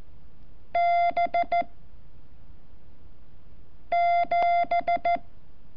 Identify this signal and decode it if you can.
Morse code, heard as a keyed tone, about 14 wpm, 690 Hz. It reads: B 7
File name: curso / ejercicio 29